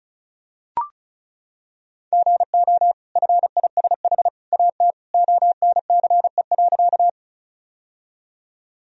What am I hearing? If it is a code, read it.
Morse code, heard as a keyed tone, about 36 wpm, 700 Hz. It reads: GO FISH AT ONCE.
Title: Sentences from Top 1000 Words - Review Part 1 - 35wpm